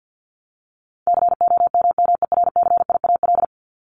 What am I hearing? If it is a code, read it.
Morse code, heard as a keyed tone, about 50 wpm, 700 Hz. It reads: COMMERCIAL
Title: Top 501-750 Words 50wpm